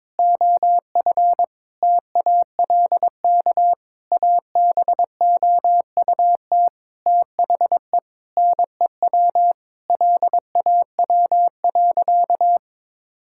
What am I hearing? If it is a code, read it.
Morse code, heard as a keyed tone, about 22 wpm, 700 Hz. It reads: OF TALK ABOUT THE NEW LAW.
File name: practice / 22wpm / Sentences with 8-Word Phrases - 22wpm